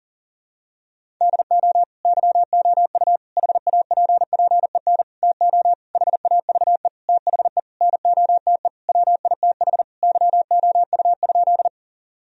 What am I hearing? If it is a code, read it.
Morse code, heard as a keyed tone, about 40 wpm, 700 Hz. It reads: DO YOU HAPPEN TO HAVE THE NOTE WITH YOU?
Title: Sentences from Top 1000 Words - Review Part 2 - 40wpm